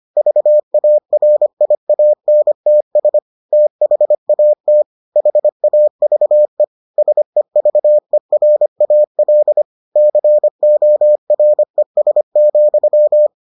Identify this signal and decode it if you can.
Morse code, heard as a keyed tone, about 25 wpm, 600 Hz. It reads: VARIANTS THAT HAVE SEVERAL CORES,